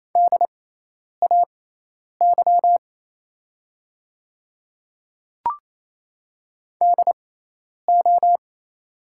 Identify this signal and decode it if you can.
Morse code, heard as a keyed tone, about 28 wpm, spaced as for 10 wpm, 700 Hz. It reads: DAY DO
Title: 6x Character Spacing - Top 100 Words - 28wpm